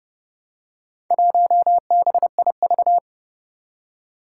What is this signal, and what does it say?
Morse code, heard as a keyed tone, about 32 wpm, 700 Hz. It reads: 1BIV
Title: Four Challenging Letter-Number Combinations 30wpm